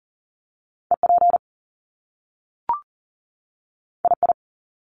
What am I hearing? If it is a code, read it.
Morse code, heard as a keyed tone, about 39 wpm, 700 Hz. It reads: EP II
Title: Top Two Letter Combinations in Words 40wpm